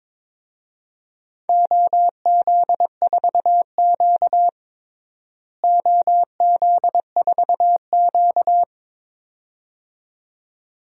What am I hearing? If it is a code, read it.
Morse code, heard as a keyed tone, about 21 wpm, 700 Hz. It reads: OZ4Q OZ4Q
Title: DXCC-100 Call Signs - 4 Characters - Repeated 2x - 22wpm